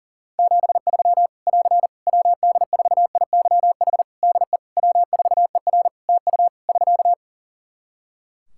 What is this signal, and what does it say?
Morse code, heard as a keyed tone, about 40 wpm, 700 Hz. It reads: 73 <AR> WD4IYH DE W4ER TU <SK>
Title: Practice QSO - Exchange 4 - Ending - 40wpm